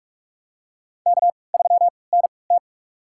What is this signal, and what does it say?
Morse code, heard as a keyed tone, about 46 wpm, spaced as for 26 wpm, 700 Hz. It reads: K3NT
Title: USA Call Signs - 4 Characters - 3x Character Spacing - 45wpm